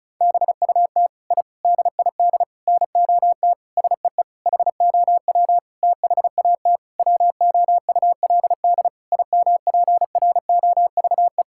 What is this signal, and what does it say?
Morse code, heard as a keyed tone, about 34 wpm, 700 Hz. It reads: BUT I DID NOT SEE HOW THAT WOULD IMPROVE
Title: Sentences from Top 1000 Words - Review Part 5 - 35wpm